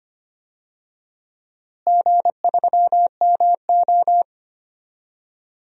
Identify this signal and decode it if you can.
Morse code, heard as a keyed tone, about 25 wpm, 700 Hz. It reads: G3MO
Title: DXCC-100 Call Signs - 4 Characters - Repeated 3x - 25wpm